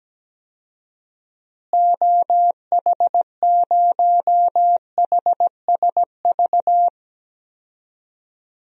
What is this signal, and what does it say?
Morse code, heard as a keyed tone, about 17 wpm, 700 Hz. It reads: OH0HSV
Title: DXCC-100 Call Signs - 6 Characters - Repeated 2x - 17wpm